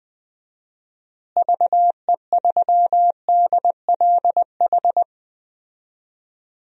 Morse code, heard as a keyed tone, about 20 wpm, 700 Hz. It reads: VE3DLH
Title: DXCC-100 Call Signs - 6 Characters - Repeated 2x - 20wpm